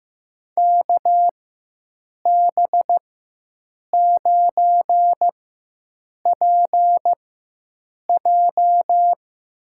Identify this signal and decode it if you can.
Morse code, heard as a keyed tone, about 15 wpm, spaced as for 7 wpm, 700 Hz. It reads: KB9PJ
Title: USA Call Signs - 6 Characters - 4x Character Spacing - 15wpm